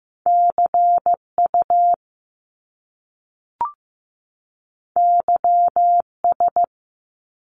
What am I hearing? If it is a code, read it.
Morse code, heard as a keyed tone, about 15 wpm, 700 Hz. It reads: CU YS
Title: Top Two Letter Combinations in Words 15wpm